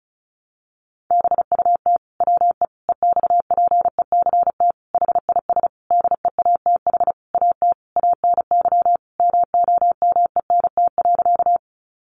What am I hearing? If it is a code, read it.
Morse code, heard as a keyed tone, about 34 wpm, 700 Hz. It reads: BUT WE EXPECT HIS DEATH AT ANY MOMENT.
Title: Sentences from Top 1100 Words 35wpm